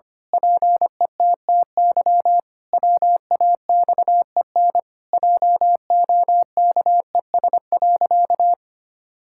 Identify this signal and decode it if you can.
Morse code, heard as a keyed tone, about 25 wpm, 700 Hz. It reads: PETTY WAXEN JOKES.
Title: Pangram Challenge v2 - 25wpm